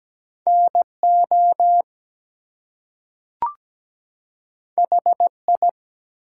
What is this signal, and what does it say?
Morse code, heard as a keyed tone, about 17 wpm, 700 Hz. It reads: NO HI
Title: Two Letter Words - 17wpm